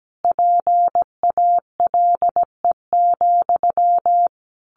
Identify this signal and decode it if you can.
Morse code, heard as a keyed tone, about 17 wpm, 700 Hz. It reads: PALE,
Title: A Sound Of Thunder 17wpm